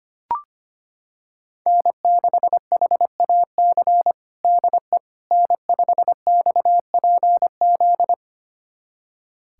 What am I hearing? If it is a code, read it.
Morse code, heard as a keyed tone, about 25 wpm, 700 Hz. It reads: N6HAC DE N5XPZ
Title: USA Callsign DE Callsign - 5 Characters - 25wpm